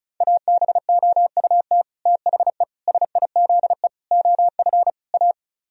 Morse code, heard as a keyed tone, about 46 wpm, 700 Hz. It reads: ABOUT THE SIZE OF A